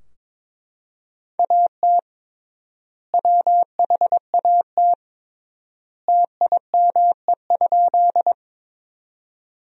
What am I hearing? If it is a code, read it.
Morse code, heard as a keyed tone, about 22 wpm, 700 Hz. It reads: AT WHAT TIME?
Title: 3x Word-Spacing - Sentences from Top 100 Words - 22wpm